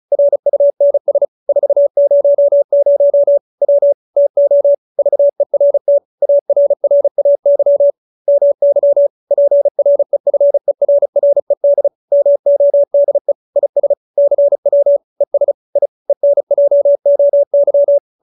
Morse code, heard as a keyed tone, about 34 wpm, 570 Hz. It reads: RUNS 400 W TO VERT ARRAY MY PREFERRED MODE IS CW ES I ENJOY